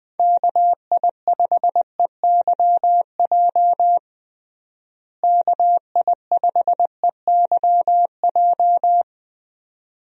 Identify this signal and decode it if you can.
Morse code, heard as a keyed tone, about 20 wpm, 700 Hz. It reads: KI5EYJ KI5EYJ
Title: USA Call Signs - 6 Characters - Repeated 2x - 20wpm